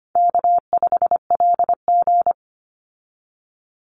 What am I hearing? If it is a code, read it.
Morse code, heard as a keyed tone, about 25 wpm, 700 Hz. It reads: K5LG
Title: Easy Call Signs Repeated 2x - 25wpm